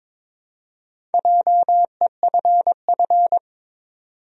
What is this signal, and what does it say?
Morse code, heard as a keyed tone, about 21 wpm, 700 Hz. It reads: JEFF
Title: USA Common Names - Max 5 Letters - 22wpm